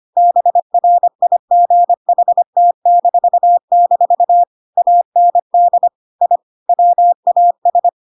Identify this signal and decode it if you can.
Morse code, heard as a keyed tone, about 25 wpm, 700 Hz. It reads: BRIGHT-- AND I WAS
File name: Pinocchio / Pinocchio ch7 25 25 ik1hge